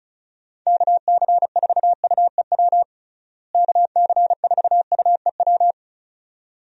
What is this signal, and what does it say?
Morse code, heard as a keyed tone, about 36 wpm, 700 Hz. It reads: KC4UEW KC4UEW
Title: USA Call Signs - 6 Characters - Repeated 2x - 35wpm